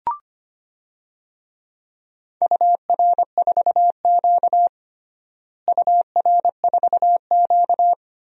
Morse code, heard as a keyed tone, about 25 wpm, 700 Hz. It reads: UR4Q UR4Q
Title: DXCC-100 Call Signs - 4 Characters - Repeated 2x - 25wpm